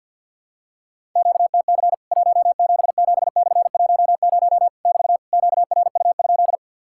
Morse code, heard as a keyed tone, about 50 wpm, 700 Hz. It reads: QTC 177/10 <BT> QRU?
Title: Worked All Europe DX Contest - QTC Exchange - 50wpm